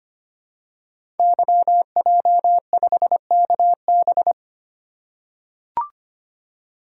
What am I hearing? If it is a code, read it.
Morse code, heard as a keyed tone, about 25 wpm, 700 Hz. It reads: YJ5KB
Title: Code Groups - 5 Letter:Numbers - 25wpm